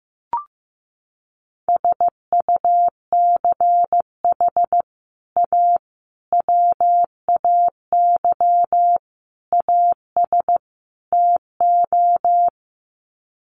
Morse code, heard as a keyed tone, about 15 wpm, 700 Hz. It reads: SUCH A WAY AS TO
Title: Sets of 5 Words Encore2 15wpm